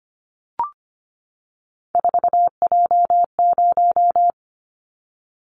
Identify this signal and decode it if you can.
Morse code, heard as a keyed tone, about 25 wpm, 700 Hz. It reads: 4J0
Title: Callsign Prefixes for Top 100 DXCC - 25wpm